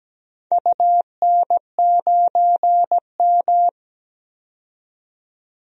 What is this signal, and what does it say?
Morse code, heard as a keyed tone, about 17 wpm, 700 Hz. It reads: UN9M
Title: DXCC-100 Call Signs - 4 Characters - 17wpm